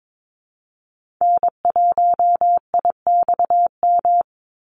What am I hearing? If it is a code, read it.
Morse code, heard as a keyed tone, about 22 wpm, 700 Hz. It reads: N1IXM
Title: USA Call Signs 22wpm